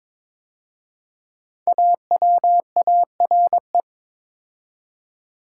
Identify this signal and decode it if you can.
Morse code, heard as a keyed tone, about 22 wpm, 700 Hz. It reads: AWARE
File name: 5 Letter Words - First 3 Letters Given - 22wpm